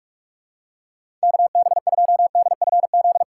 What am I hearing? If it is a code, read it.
Morse code, heard as a keyed tone, about 46 wpm, 700 Hz. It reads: KB2DFZ
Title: USA Call Signs - 6 Characters - 4x Character Spacing - 45wpm